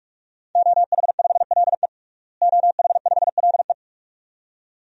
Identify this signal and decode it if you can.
Morse code, heard as a keyed tone, about 46 wpm, 700 Hz. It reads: OH5LE OH5LE
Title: DXCC-100 Call Signs - 5 Characters - Repeated 2x - 45wpm